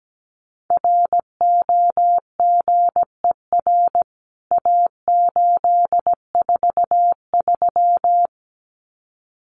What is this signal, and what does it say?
Morse code, heard as a keyed tone, about 17 wpm, 700 Hz. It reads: ROGER A843
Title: CWT Exchange 17wpm